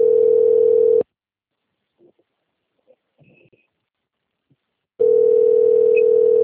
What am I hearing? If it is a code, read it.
Ringback tone (North America), 440 + 480 Hz.